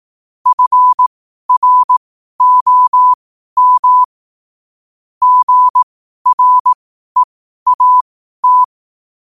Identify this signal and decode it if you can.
Morse code, heard as a keyed tone, about 18 wpm, spaced as for 12 wpm, 1000 Hz. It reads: FROM GREAT
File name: qotdfiles / 12wpm